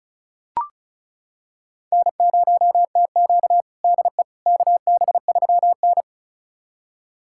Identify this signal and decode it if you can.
Morse code, heard as a keyed tone, about 36 wpm, 700 Hz. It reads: N0TQ DE KB3N
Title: USA Callsign DE Callsign - 4 Characters - 35wpm